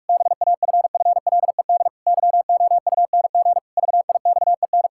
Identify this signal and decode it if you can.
Morse code, heard as a keyed tone, about 41 wpm, 700 Hz. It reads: BAFFLED YOUNG VIXEN
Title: Pangram Challenge v2 - 45wpm